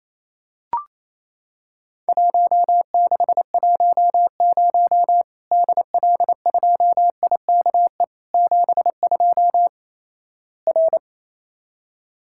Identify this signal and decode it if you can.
Morse code, heard as a keyed tone, about 28 wpm, 700 Hz. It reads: 1610 DL2IKE 72 R
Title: Worked All Europe DX Contest - QTC Exchange - 28wpm